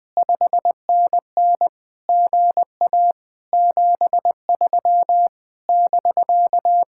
Morse code, heard as a keyed tone, about 20 wpm, 700 Hz. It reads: 5NN GA 73 <BK>
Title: Speed-Racing Common QSO Phrases Encore - 20wpm